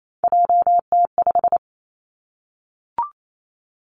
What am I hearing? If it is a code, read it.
Morse code, heard as a keyed tone, about 28 wpm, 700 Hz. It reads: JT5
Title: Three Challenging Letter-Number Combinations 28wpm